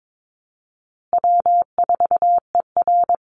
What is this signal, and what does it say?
Morse code, heard as a keyed tone, about 22 wpm, 700 Hz. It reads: W4ER
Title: Easy Call Signs Repeated 3x - 22wpm